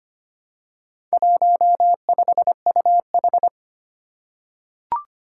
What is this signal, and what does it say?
Morse code, heard as a keyed tone, about 25 wpm, 700 Hz. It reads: 15UH
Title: Four Challenging Letter-Number Combinations 25wpm